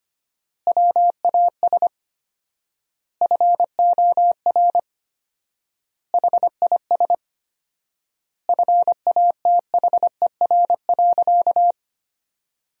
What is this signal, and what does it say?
Morse code, heard as a keyed tone, about 25 wpm, 700 Hz. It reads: WAS FOR HIS FATHER.
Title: 4x Word-Spacing - Sentences from Top 500 Words - 25wpm